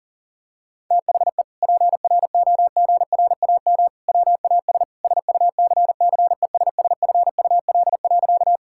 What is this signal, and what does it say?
Morse code, heard as a keyed tone, about 40 wpm, 700 Hz. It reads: THE PROGRAM WAS SUCCESSFUL.
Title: Sentences from Top 1000 Words - Review Part 5 - 40wpm